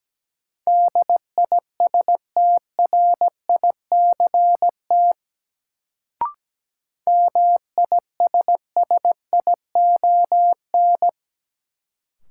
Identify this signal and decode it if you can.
Morse code, heard as a keyed tone, about 17 wpm, 700 Hz. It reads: DISTRICT MISSION
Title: Sentences from Top 1600 Words - New Word Drill Alt - 17wpm